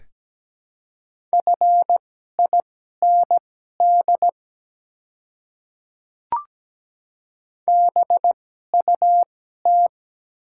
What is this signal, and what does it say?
Morse code, heard as a keyed tone, about 17 wpm, spaced as for 12 wpm, 690 Hz. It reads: FIND BUT
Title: 2x Character Spacing - Top 100 Words - 17wpm